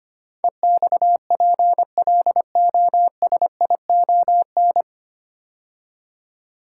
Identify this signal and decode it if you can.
Morse code, heard as a keyed tone, about 25 wpm, 700 Hz. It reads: EXPLOSION
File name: practice / 25wpm / 9 Letter Words - First 3 Letters Given - 25wpm